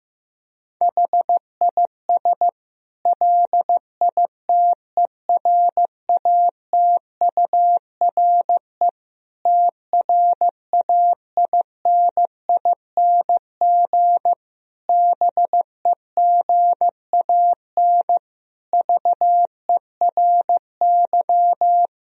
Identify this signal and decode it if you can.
Morse code, heard as a keyed tone, about 15 wpm, 700 Hz. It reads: HIS LITERATURE TRAINING BEGAN VERY
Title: Sentences from Top 1000 Words - Review Part 4 - 15wpm